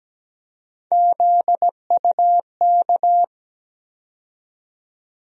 Morse code, heard as a keyed tone, about 17 wpm, 700 Hz. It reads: ZUK
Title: USA Call Sign Postfixes - 3 Letters 17wpm